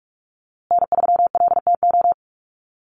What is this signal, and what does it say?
Morse code, heard as a keyed tone, about 44 wpm, 700 Hz. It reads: N3LTO
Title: USA Call Signs Repeated 3x 45wpm